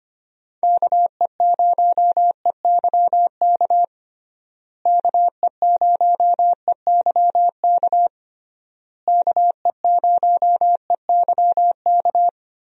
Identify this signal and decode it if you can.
Morse code, heard as a keyed tone, about 25 wpm, 700 Hz. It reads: KE0EYK KE0EYK KE0EYK
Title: USA Call Signs - 6 Characters - Repeated 3x - 25wpm